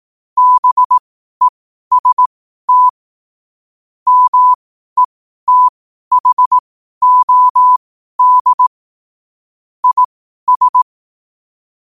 Morse code, heard as a keyed tone, about 18 wpm, spaced as for 12 wpm, 1000 Hz. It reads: BEST METHOD IS
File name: qotdfiles / 12wpm